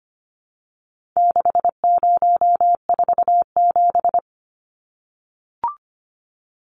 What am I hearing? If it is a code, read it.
Morse code, heard as a keyed tone, about 25 wpm, 700 Hz. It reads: 6047
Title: Numbers - 4-Digit 25wpm